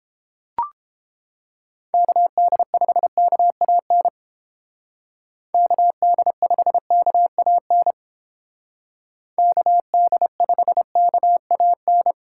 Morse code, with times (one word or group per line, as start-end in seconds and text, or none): 1.9-4.1 KD5KAN
5.5-7.9 KD5KAN
9.4-12.1 KD5KAN